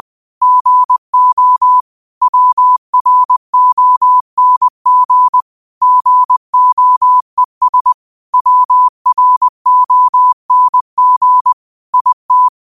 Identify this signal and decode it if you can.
Morse code, heard as a keyed tone, about 20 wpm, 1000 Hz. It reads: GO WRONG GOES WRONG IT
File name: qotdfiles / 20wpm